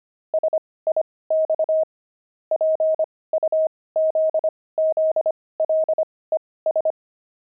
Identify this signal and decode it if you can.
Morse code, heard as a keyed tone, about 25 wpm, spaced as for 18 wpm, 630 Hz. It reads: SIX PUZZLES
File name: pangram-fs-18-cs-25-puzzles